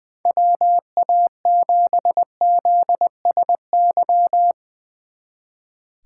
Morse code, heard as a keyed tone, about 22 wpm, 700 Hz. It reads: WA7ZSY
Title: Speed Racing - USA 2x3 Call Signs 20wpm